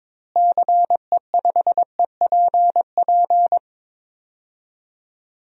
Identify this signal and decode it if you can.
Morse code, heard as a keyed tone, about 22 wpm, 700 Hz. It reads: CE5EPP
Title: DXCC-100 Call Signs - 6 Characters - Repeated 3x - 22wpm